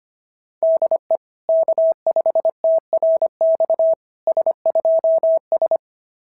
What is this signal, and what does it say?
Morse code, heard as a keyed tone, about 22 wpm, 650 Hz. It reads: DE K5TRX S2S
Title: SOTA QSO Practice 25wpm